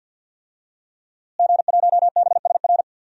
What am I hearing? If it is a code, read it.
Morse code, heard as a keyed tone, about 52 wpm, 700 Hz. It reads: G1BSR
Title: DXCC-100 Call Signs - 5 Characters - Repeated 3x - 50wpm